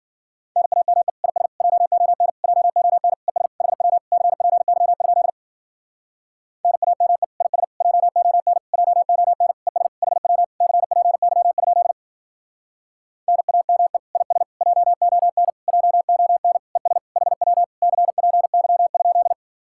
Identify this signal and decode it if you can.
Morse code, heard as a keyed tone, about 52 wpm, 700 Hz. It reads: NAME IS JON JON ES HW CPY? NAME IS JON JON ES HW CPY? NAME IS JON JON ES HW CPY?